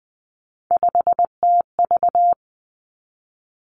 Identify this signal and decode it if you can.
Morse code, heard as a keyed tone, about 19 wpm, 700 Hz. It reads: 5TV
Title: Three Challenging Letter-Number Combinations 20wpm